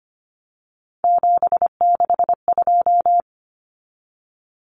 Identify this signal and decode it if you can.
Morse code, heard as a keyed tone, about 25 wpm, 700 Hz. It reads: 762
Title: Common Cut Numbers - 3-Digit 25wpm